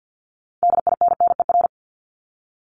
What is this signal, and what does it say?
Morse code, heard as a keyed tone, about 50 wpm, 690 Hz. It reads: DINNER
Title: Multi-letter Phonemes Repeated 3x 50wpm